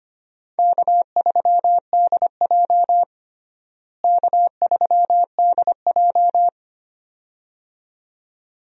Morse code, heard as a keyed tone, about 25 wpm, 700 Hz. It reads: K3DJ K3DJ
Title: USA Call Signs - 4 Characters - Repeated 2x - 25wpm